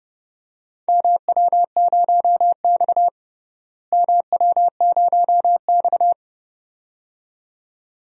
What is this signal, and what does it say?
Morse code, heard as a keyed tone, about 30 wpm, 700 Hz. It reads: MW0X MW0X
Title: DXCC-100 Call Signs - 4 Characters - Repeated 2x - 30wpm